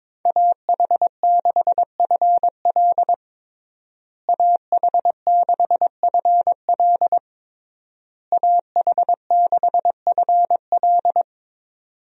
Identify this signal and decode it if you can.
Morse code, heard as a keyed tone, about 22 wpm, 700 Hz. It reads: AH6FL AH6FL AH6FL
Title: USA Call Signs - 5 Characters - Repeated 3x - 22wpm